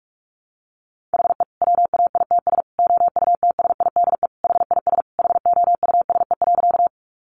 Morse code, heard as a keyed tone, about 44 wpm, 700 Hz. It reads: HE WAITS OUTSIDE HIS HOUSE.
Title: Additional Easy Phrases 45wpm